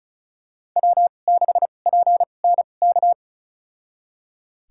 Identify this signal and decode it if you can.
Morse code, heard as a keyed tone, about 34 wpm, spaced as for 25 wpm, 700 Hz. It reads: W6PNK
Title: USA Call Signs - 5 Characters - 2x Character Spacing - 35wpm